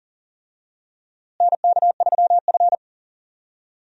Morse code, heard as a keyed tone, about 39 wpm, 700 Hz. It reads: NK3F
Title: USA Call Signs - 4 Characters - Repeated 3x - 40wpm